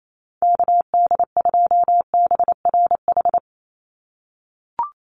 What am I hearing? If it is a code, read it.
Morse code, heard as a keyed tone, about 28 wpm, 700 Hz. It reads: KD2BRH
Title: USA Call Signs 28wpm